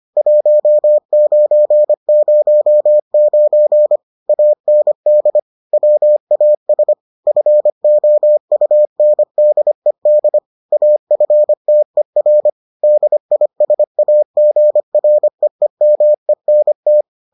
Morse code, heard as a keyed tone, about 25 wpm, 600 Hz. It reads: 1909 AND WAS FOUNDED AFTER DISAGREEMENT